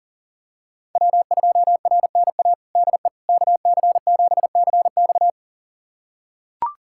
Morse code, heard as a keyed tone, about 37 wpm, 700 Hz. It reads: W2RNA DE KC7CX
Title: USA Callsign DE Callsign - 5 Characters - 40wpm